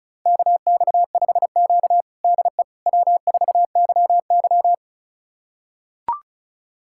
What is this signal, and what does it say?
Morse code, heard as a keyed tone, about 36 wpm, 700 Hz. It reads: KX5Q DE W4YY
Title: USA Callsign DE Callsign - 4 Characters - 35wpm